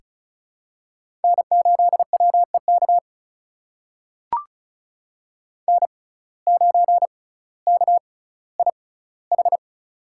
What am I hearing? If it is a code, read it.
Morse code, heard as a keyed tone, about 36 wpm, spaced as for 18 wpm, 700 Hz. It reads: N8WEK N9KIH
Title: USA Call Signs - 5 Characters - 6x Character Spacing - 35wpm